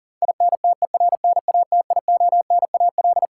Morse code, heard as a keyed tone, about 40 wpm, 700 Hz. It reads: INTERNATIONAL
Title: Sentences from Top 1000 Words - Review Part 4 - New Word Drill - 40wpm